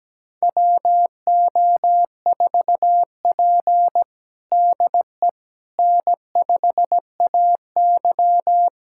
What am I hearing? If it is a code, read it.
Morse code, heard as a keyed tone, about 17 wpm, 700 Hz. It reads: WO4P DE N5AY
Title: USA Callsign DE Callsign - 4 Characters - 17wpm